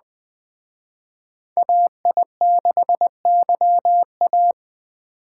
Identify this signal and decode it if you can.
Morse code, heard as a keyed tone, about 21 wpm, 700 Hz. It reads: AI6YA